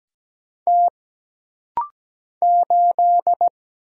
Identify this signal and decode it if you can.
Morse code, heard as a keyed tone, about 17 wpm, 700 Hz. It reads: T 8
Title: Single Letter-Number Alt - Rapid-Fire 17wpm